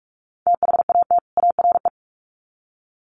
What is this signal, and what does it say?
Morse code, heard as a keyed tone, about 46 wpm, 700 Hz. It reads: THAT ARE
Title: Sets of 2 Words 45wpm